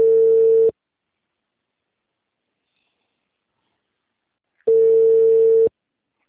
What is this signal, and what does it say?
Ringback tone (China, 450 Hz).